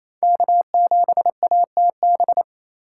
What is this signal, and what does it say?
Morse code, heard as a keyed tone, about 28 wpm, 700 Hz. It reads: K7ATB
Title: USA Call Signs - 5 Characters - 2x Character Spacing - 28wpm